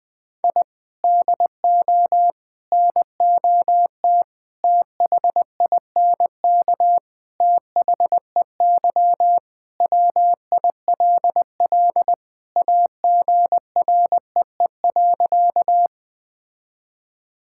Morse code, heard as a keyed tone, about 21 wpm, 700 Hz. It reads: I DO NOT THINK THEY WILL AGREE.
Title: Sentences with 5-Word Phrases - 20wpm